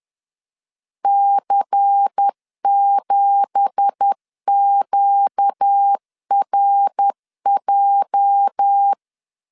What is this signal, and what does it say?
Morse code, heard as a keyed tone, about 10 wpm, 800 Hz. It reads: C7QRJ
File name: LEZIONI / 9 2867 10wpm 2733